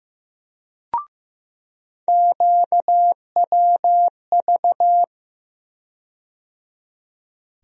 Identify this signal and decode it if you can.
Morse code, heard as a keyed tone, about 15 wpm, 700 Hz. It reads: QWV
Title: USA Call Sign Postfixes - 3 Letters 15wpm